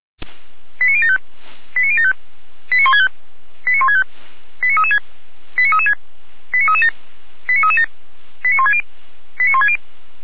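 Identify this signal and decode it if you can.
Five-tone selective call (ZVEI), heard as quick runs of five tones, tones of about 70 ms. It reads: 89075 89075 89165 89165 89336 89336 89337 89337 89160 89160
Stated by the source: Five-tone ZVEI selective call received by SDR.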